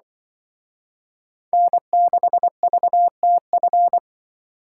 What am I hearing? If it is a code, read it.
Morse code, heard as a keyed tone, about 24 wpm, 700 Hz. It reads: N6VTF
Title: Speed Racing - USA 1x3 Call Signs 20wpm